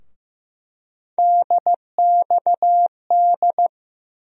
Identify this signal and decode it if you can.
Morse code, heard as a keyed tone, about 15 wpm, 700 Hz. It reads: DXD
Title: USA Call Sign Postfixes - 3 Letters 15wpm